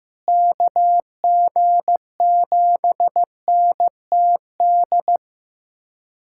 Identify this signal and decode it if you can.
Morse code, heard as a keyed tone, about 15 wpm, 700 Hz. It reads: KG7NTD
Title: USA Callsign DE Callsign - 6 Characters - 15wpm